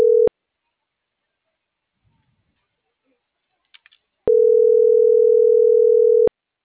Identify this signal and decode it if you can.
Ringback tone (North America), 440 + 480 Hz.